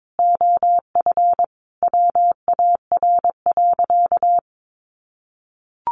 Morse code, heard as a keyed tone, about 22 wpm, 700 Hz. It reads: OF WAR.
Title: Sentences from Top 1300 Words 22wpm